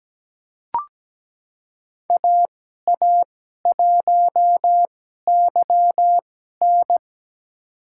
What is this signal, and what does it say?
Morse code, heard as a keyed tone, about 17 wpm, spaced as for 12 wpm, 700 Hz. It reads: AA1YN
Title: USA Call Signs - 5 Characters - 2x Character Spacing - 17wpm